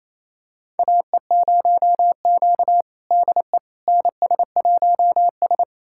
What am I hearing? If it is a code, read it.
Morse code, heard as a keyed tone, about 28 wpm, 700 Hz. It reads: AE0Q DE NS1S
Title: USA Callsign DE Callsign - 4 Characters - 28wpm